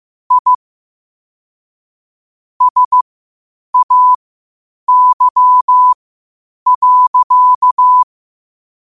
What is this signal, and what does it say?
Morse code, heard as a keyed tone, about 15 wpm, spaced as for 8 wpm, 1000 Hz. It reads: I SAY.
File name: qotdfiles / 8wpm